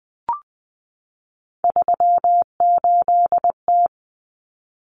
Morse code, heard as a keyed tone, about 20 wpm, 700 Hz. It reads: 38T
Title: Common Cut Numbers - 3-Digit 20wpm